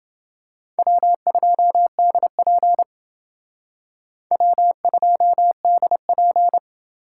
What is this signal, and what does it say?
Morse code, heard as a keyed tone, about 30 wpm, 700 Hz. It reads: W2DP W2DP